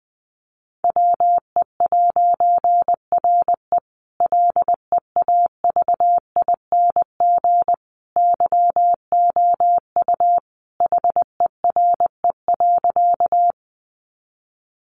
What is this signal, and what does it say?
Morse code, heard as a keyed tone, about 20 wpm, 700 Hz. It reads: WE'RE LEAVING YOU HERE.
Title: A Sound Of Thunder 20wpm